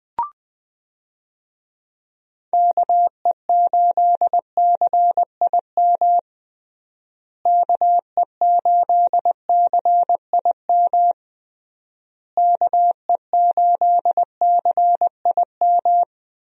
Morse code, heard as a keyed tone, about 20 wpm, 700 Hz. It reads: KE8CIM KE8CIM KE8CIM
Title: USA Call Signs - 6 Characters - Repeated 3x - 20wpm